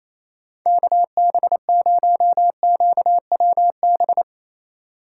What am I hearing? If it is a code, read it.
Morse code, heard as a keyed tone, about 37 wpm, 700 Hz. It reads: KB0QWB